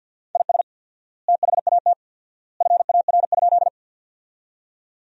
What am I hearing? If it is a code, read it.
Morse code, heard as a keyed tone, about 48 wpm, 700 Hz. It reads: IS THAT FAR?
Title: 4x Word-Spacing - Sentences from Top 300 Words - 50wpm